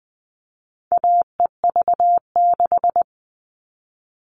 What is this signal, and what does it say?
Morse code, heard as a keyed tone, about 20 wpm, 700 Hz. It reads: AEV6
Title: All Cut Numbers - 4-Digit 20wpm